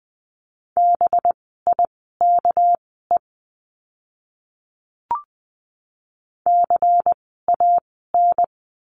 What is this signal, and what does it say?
Morse code, heard as a keyed tone, about 20 wpm, spaced as for 14 wpm, 700 Hz. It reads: BIKE CAN
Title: 199 Lesson 32f - Letter k - Words 20wpm